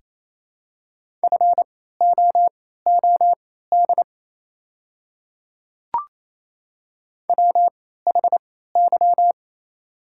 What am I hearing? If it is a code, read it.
Morse code, heard as a keyed tone, about 28 wpm, spaced as for 16 wpm, 700 Hz. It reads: FOOD WHY
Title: 3x Character Spacing - Top 500 Words - 28wpm